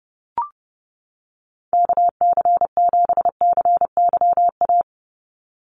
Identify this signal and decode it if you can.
Morse code, heard as a keyed tone, about 30 wpm, 700 Hz. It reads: KC7CYA
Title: USA Call Signs 30wpm